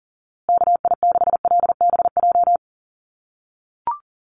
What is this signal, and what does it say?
Morse code, heard as a keyed tone, about 40 wpm, 700 Hz. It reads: KI6LBJ
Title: USA Call Signs Repeated 2x 40wpm